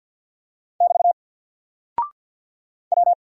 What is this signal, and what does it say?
Morse code, heard as a keyed tone, about 50 wpm, 700 Hz. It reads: <BT> W
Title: Single Character (Letter:Number:Punctuation:Prosign) 50wpm